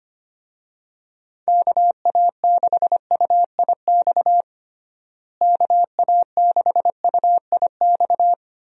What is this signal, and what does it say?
Morse code, heard as a keyed tone, about 25 wpm, 700 Hz. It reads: KA6UIX KA6UIX
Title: USA Call Signs - 6 Characters - Repeated 2x - 25wpm